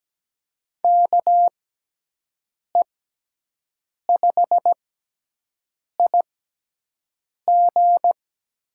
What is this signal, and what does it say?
Morse code, heard as a keyed tone, about 17 wpm, spaced as for 6 wpm, 700 Hz. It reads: KE5IG
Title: USA Call Signs - 6 Characters - 6x Character Spacing - 17wpm